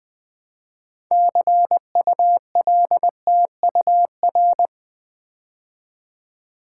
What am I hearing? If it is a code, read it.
Morse code, heard as a keyed tone, about 20 wpm, 700 Hz. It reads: CULTUR
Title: Build A Word - 3 Letters At A Time - 8-Letter Words - 20wpm